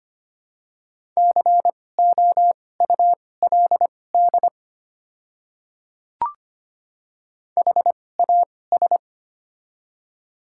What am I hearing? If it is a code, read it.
Morse code, heard as a keyed tone, about 25 wpm, spaced as for 18 wpm, 700 Hz. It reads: COULD HAS
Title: 2x Character Spacing - Top 200 Words - 25wpm